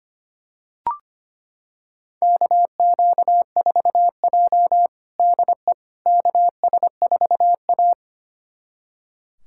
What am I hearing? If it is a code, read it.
Morse code, heard as a keyed tone, about 25 wpm, 700 Hz. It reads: KQ4J DE KS4A
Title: USA Callsign DE Callsign - 4 Characters - 25wpm